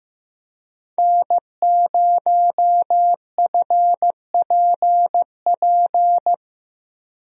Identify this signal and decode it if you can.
Morse code, heard as a keyed tone, about 15 wpm, 700 Hz. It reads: N0FPP
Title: USA Call Signs - 5 Characters - 15wpm